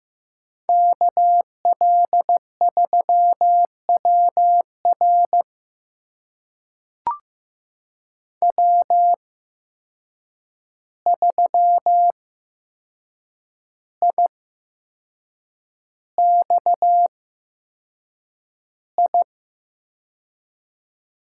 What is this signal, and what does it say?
Morse code, heard as a keyed tone, about 15 wpm, spaced as for 6 wpm, 700 Hz. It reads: KL3WR W3IXI
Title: USA Call Signs - 5 Characters - 8x Character Spacing - 15wpm